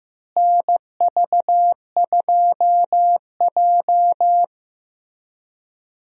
Morse code, heard as a keyed tone, about 16 wpm, 700 Hz. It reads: NV2J
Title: USA Call Signs - 4 Characters - Repeated 3x - 15wpm